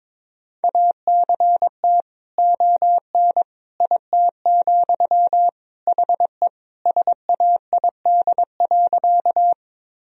Morse code, heard as a keyed tone, about 22 wpm, 700 Hz. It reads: ACT ON IT, HE SAID.
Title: Sentences from Top 1000 Words - Review Part 1 - 22wpm